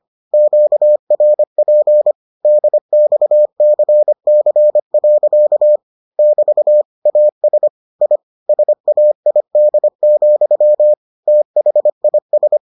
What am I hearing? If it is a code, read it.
Morse code, heard as a keyed tone, about 25 wpm, 600 Hz. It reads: QRP DXCC. <BT> AS I SAID, THIS